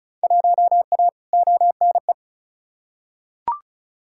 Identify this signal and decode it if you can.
Morse code, heard as a keyed tone, about 36 wpm, 700 Hz. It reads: 1A ONE
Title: Field Day Exchanges Encore - 35wpm